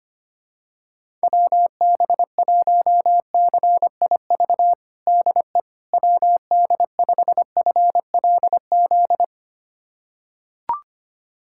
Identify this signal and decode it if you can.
Morse code, heard as a keyed tone, about 25 wpm, 700 Hz. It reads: WB1CIV DE WD5FLZ
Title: USA Callsign DE Callsign - 6 Characters - 25wpm